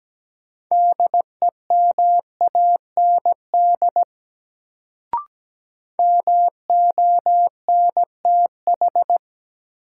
Morse code, heard as a keyed tone, about 17 wpm, 700 Hz. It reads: DEMAND MONTH
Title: Sentences from Top 800 Words - New Word Drill - 17wpm